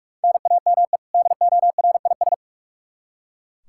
Morse code, heard as a keyed tone, about 45 wpm, 700 Hz. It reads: NAME DORIS
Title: Practice QSO - Exchange 1 - Beginning - 45wpm